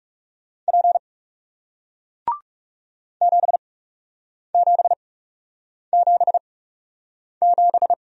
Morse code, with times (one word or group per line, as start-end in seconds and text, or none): 0.7-1.0 P
3.2-7.9 7777